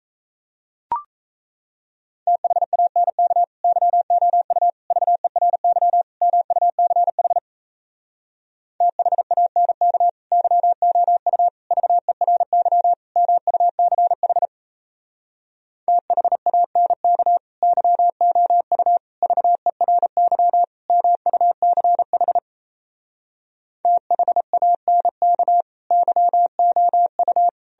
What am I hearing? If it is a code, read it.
Morse code, heard as a keyed tone, about 38 wpm, 700 Hz. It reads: THANK YOU VERY MUCH THANK YOU VERY MUCH THANK YOU VERY MUCH THANK YOU